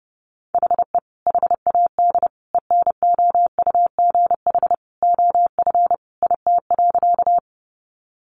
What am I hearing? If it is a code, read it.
Morse code, heard as a keyed tone, about 28 wpm, 700 Hz. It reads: HE HAD ENOUGH OF IT.
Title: Sentences from Top 400 Words 30wpm